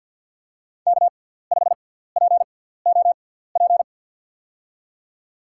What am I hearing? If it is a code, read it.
Morse code, heard as a keyed tone, about 52 wpm, spaced as for 17 wpm, 690 Hz. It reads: K5POP
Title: USA Call Signs - 6x Character Spacing - 50wpm